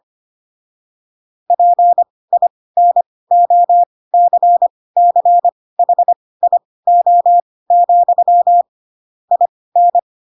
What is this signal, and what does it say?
Morse code, heard as a keyed tone, about 25 wpm, spaced as for 18 wpm, 700 Hz. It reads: PINOCCHIO, IN
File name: Pinocchio / Pinocchio ch6 12 25 ik1hge